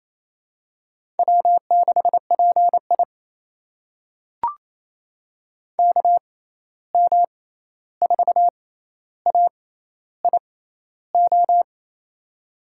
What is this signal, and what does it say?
Morse code, heard as a keyed tone, about 28 wpm, spaced as for 13 wpm, 700 Hz. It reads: W6PI KM4AIO
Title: USA Call Signs - 6x Character Spacing - 28wpm